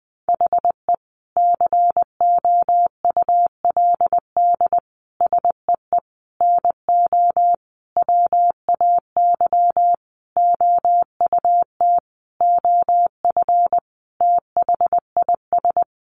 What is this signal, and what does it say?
Morse code, heard as a keyed tone, about 20 wpm, 700 Hz. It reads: HE COULD SEE NO WAY OUT OF THIS
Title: Sentences from Top 1200 Words 20wpm